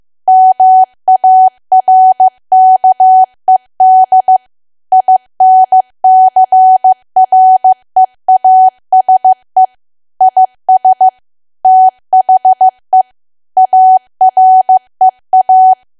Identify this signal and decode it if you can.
Morse code, heard as a keyed tone, about 15 wpm, 750 Hz. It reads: MARKED INCREASE IS THE AREA
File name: cwa / W1AW-Level2 / 15WPM